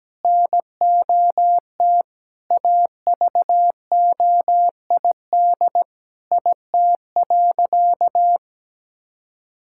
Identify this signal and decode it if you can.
Morse code, heard as a keyed tone, about 17 wpm, 700 Hz. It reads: NOT AVOID IT.
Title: Sentences from Top 1000 Words - Review Part 6 - 17wpm